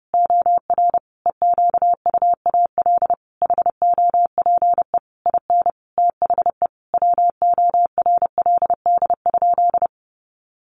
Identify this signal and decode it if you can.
Morse code, heard as a keyed tone, about 30 wpm, 700 Hz. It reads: OR EQUAL HOPE IN THE WORLD?
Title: Sentences from Top 1400 Words 30wpm